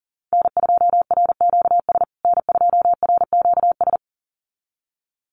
Morse code, heard as a keyed tone, about 37 wpm, 700 Hz. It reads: N2RQS N2RQS
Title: USA Call Signs Repeated 2x 40wpm